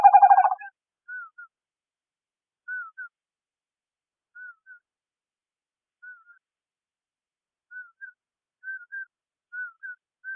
There is no code.